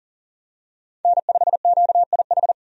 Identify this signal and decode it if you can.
Morse code, heard as a keyed tone, about 40 wpm, 700 Hz. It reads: N5QIH
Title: USA Call Signs - 5 Characters - 40wpm